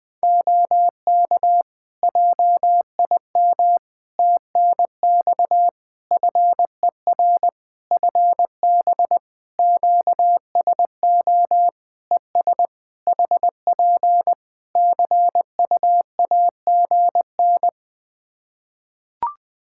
Morse code, heard as a keyed tone, about 20 wpm, 700 Hz. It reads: OK JIM TNX FER FB QSO ES HP CU AGN
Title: Practice QSO - Exchange 4 - Ending - 20wpm